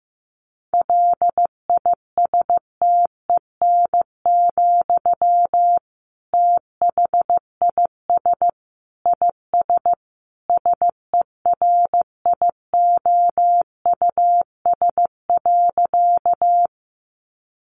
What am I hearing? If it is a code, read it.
Morse code, heard as a keyed tone, about 15 wpm, 700 Hz. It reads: LISTEN, THIS IS SERIOUS.
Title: Sentences from Top 1100 Words 15wpm